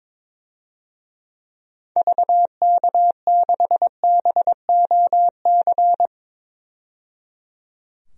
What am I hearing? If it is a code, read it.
Morse code, heard as a keyed tone, about 22 wpm, 700 Hz. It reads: VK6BOC
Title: DXCC-100 Call Signs - 6 Characters - 22wpm